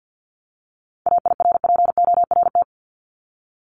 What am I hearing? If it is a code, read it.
Morse code, heard as a keyed tone, about 50 wpm, 700 Hz. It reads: AIRPORT
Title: Top 1751-2000 Words 50wpm